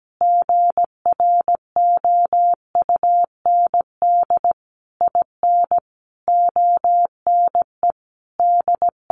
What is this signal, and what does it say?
Morse code, heard as a keyed tone, about 17 wpm, 700 Hz. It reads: GROUND IN ONE D
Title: Sentences from Top 500 Words 17wpm